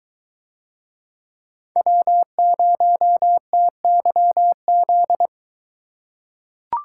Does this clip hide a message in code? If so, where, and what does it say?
Yes: Morse code from 1.8 to 5.3 s, heard as a keyed tone, about 22 wpm, 700 Hz. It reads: W0TYZ